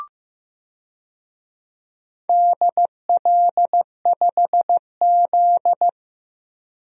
Morse code, heard as a keyed tone, about 15 wpm, 700 Hz. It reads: DL5Z
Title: DXCC-100 Call Signs - 4 Characters - Repeated 2x - 15wpm